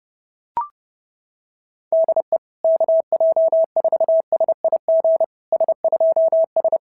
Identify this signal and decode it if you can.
Morse code, heard as a keyed tone, about 30 wpm, 650 Hz. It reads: DE KJ4SIG S2S
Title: SOTA QSO Practice 30wpm